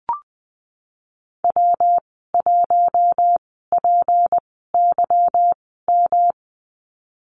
Morse code, heard as a keyed tone, about 19 wpm, spaced as for 14 wpm, 700 Hz. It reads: W1PYM
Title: 154 Lesson 25g - Letter p - Call Signs 20wpm